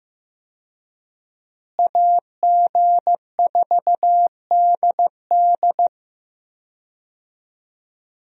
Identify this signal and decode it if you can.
Morse code, heard as a keyed tone, about 15 wpm, 700 Hz. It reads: AG4DD
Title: USA Call Signs - 5 Characters - Repeated 3x - 15wpm